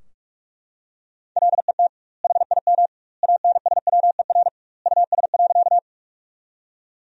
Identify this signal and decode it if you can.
Morse code, heard as a keyed tone, about 42 wpm, 700 Hz. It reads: LET HIM ANSWER US.
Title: 2x Word-Spacing - Sentences from Top 300 Words - 45wpm